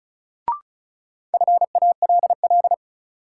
Morse code, heard as a keyed tone, about 36 wpm, 700 Hz. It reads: FALL
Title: Sentences from Top 500 Words - New Word Drill - 35wpm